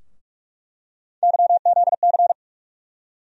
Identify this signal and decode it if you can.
Morse code, heard as a keyed tone, about 48 wpm, 700 Hz. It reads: YZC
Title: USA Call Sign Postfixes - 3 Letters 45wpm